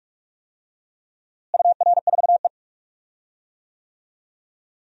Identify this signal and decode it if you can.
Morse code, heard as a keyed tone, about 46 wpm, 700 Hz. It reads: UR4E
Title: DXCC-100 Call Signs - 4 Characters - Repeated 2x - 45wpm